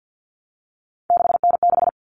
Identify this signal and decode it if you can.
Morse code, heard as a keyed tone, about 50 wpm, 700 Hz. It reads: 6N6
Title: All Cut Numbers - 3-Digit 50wpm